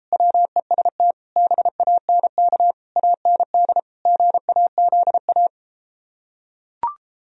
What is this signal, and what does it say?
Morse code, heard as a keyed tone, about 33 wpm, 700 Hz. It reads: WEST BANK AND GAZA